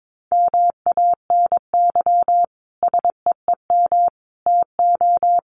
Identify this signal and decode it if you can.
Morse code, heard as a keyed tone, about 22 wpm, 700 Hz. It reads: MANY SEEM TO
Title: Sentences from Top 1400 Words 22wpm